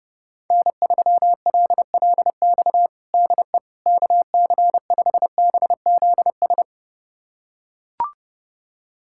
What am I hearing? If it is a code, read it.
Morse code, heard as a keyed tone, about 30 wpm, 700 Hz. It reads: N3LLX DE KC5BZS
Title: USA Callsign DE Callsign 30wpm